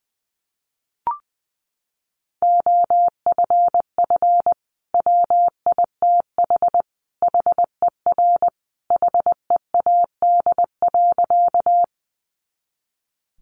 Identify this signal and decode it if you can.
Morse code, heard as a keyed tone, about 20 wpm, 700 Hz. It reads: OFF WITH HER HEAD.
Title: Sentences from Top 300 Words 20wpm